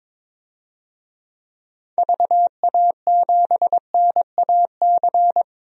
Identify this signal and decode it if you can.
Morse code, heard as a keyed tone, about 21 wpm, 700 Hz. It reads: VA7NAC
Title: DXCC-100 Call Signs - 6 Characters - 22wpm